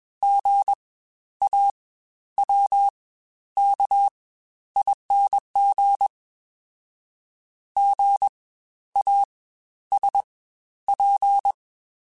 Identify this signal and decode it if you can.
Morse code, heard as a keyed tone, about 21 wpm, spaced as for 11 wpm, 800 Hz. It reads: GAWKING GASP
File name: Spaced Alphas